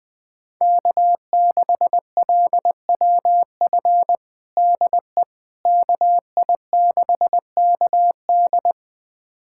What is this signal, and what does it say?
Morse code, heard as a keyed tone, about 21 wpm, 700 Hz. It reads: K6LWF DE KI6KD
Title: USA Callsign DE Callsign - 5 Characters - 20wpm